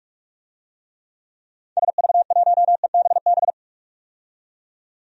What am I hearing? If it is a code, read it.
Morse code, heard as a keyed tone, about 43 wpm, 700 Hz. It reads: SV1EBB
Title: DXCC-100 Call Signs - 6 Characters - 45wpm